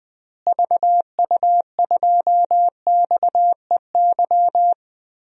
Morse code, heard as a keyed tone, about 20 wpm, 700 Hz. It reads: VU2XEY
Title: DXCC-100 Call Signs - 6 Characters - Repeated 2x - 20wpm